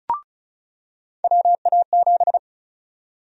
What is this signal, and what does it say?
Morse code, heard as a keyed tone, about 34 wpm, 700 Hz. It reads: WA7
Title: USA Call Sign Prefixes - 35wpm